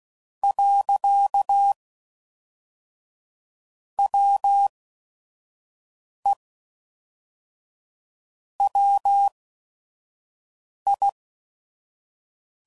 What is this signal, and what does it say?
Morse code, heard as a keyed tone, about 16 wpm, spaced as for 5 wpm, 800 Hz. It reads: . WE WI
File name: CW Code Groups @5WPM